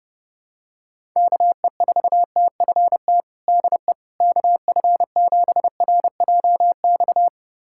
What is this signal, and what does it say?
Morse code, heard as a keyed tone, about 30 wpm, 700 Hz. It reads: KE4TFT DE KF7RJX
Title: USA Callsign DE Callsign - 6 Characters - 30wpm